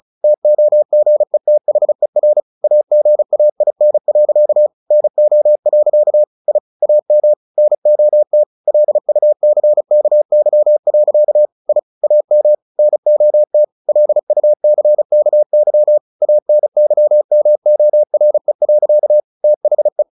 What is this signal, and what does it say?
Morse code, heard as a keyed tone, about 34 wpm, 600 Hz. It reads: TOGETHER AGAIN. NO. I AM NOT LUCKY. I AM NOT LUCKY ANYMORE. THE
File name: the old man and the sea 2 35wpm